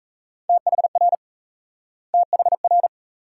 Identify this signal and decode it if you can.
Morse code, heard as a keyed tone, about 38 wpm, 700 Hz. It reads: THR THR